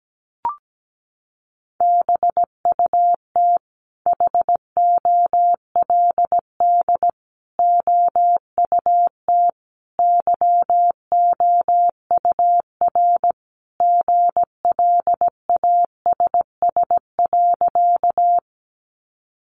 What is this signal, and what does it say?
Morse code, heard as a keyed tone, about 17 wpm, 700 Hz. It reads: BUT HOLD OUT YOUR GLASS.
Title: Sentences from Top 1300 Words 17wpm